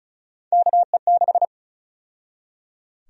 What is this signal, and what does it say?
Morse code, heard as a keyed tone, about 34 wpm, 700 Hz. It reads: KE6
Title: USA Call Sign Prefixes - 35wpm